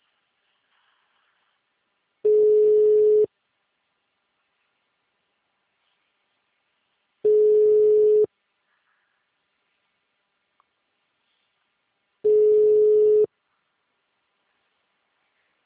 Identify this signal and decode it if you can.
Ringback tone (Europe, 425 Hz).